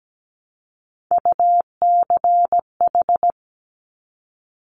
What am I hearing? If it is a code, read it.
Morse code, heard as a keyed tone, about 17 wpm, 700 Hz. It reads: UCH
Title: Top Three Letter Combinations in Words 17wpm